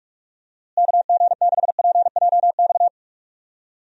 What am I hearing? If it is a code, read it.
Morse code, heard as a keyed tone, about 42 wpm, 700 Hz. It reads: KG6PJX
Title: USA Call Signs - 6 Characters - Repeated 2x - 45wpm